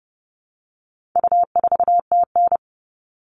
Morse code, heard as a keyed tone, about 30 wpm, 700 Hz. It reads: U4TN
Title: All Cut Numbers - 4-Digit 30wpm